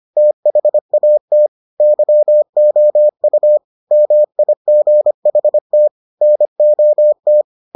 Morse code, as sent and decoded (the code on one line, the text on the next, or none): - .... .- - / -.-- --- ..- / -- .. --. .... - / -. --- -
THAT YOU MIGHT NOT